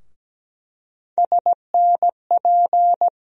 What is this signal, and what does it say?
Morse code, heard as a keyed tone, about 17 wpm, 700 Hz. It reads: SNP
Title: USA Call Sign Postfixes - 3 Letters 17wpm